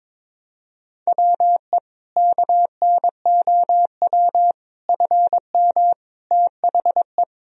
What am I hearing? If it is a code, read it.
Morse code, heard as a keyed tone, about 22 wpm, 700 Hz. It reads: WE KNOW FM THE
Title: Sentences from Top 500 Words and Abbreviations 22wpm